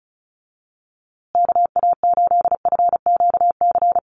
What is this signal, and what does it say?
Morse code, heard as a keyed tone, about 34 wpm, 700 Hz. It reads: KA8FQC
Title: USA Call Signs Repeated 3x 35wpm